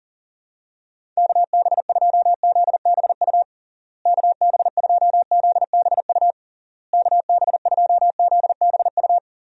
Morse code, heard as a keyed tone, about 40 wpm, 700 Hz. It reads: KB2ZBU KB2ZBU KB2ZBU
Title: USA Call Signs - 6 Characters - Repeated 3x - 40wpm